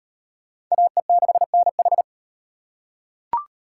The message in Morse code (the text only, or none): AE6NH